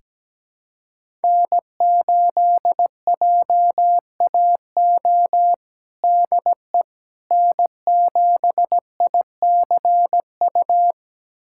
Morse code, heard as a keyed tone, about 17 wpm, 700 Hz. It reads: N8JAO DE N7ICU
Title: USA Callsign DE Callsign 17wpm